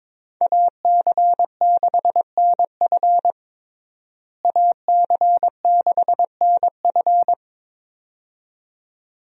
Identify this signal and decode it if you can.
Morse code, heard as a keyed tone, about 22 wpm, 700 Hz. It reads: AC6NF AC6NF
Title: USA Call Signs - 5 Characters - Repeated 2x - 22wpm